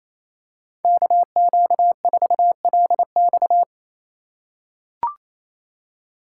Morse code, heard as a keyed tone, about 28 wpm, 700 Hz. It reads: KQ4LX
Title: USA Call Signs - 5 Characters - 2x Character Spacing - 28wpm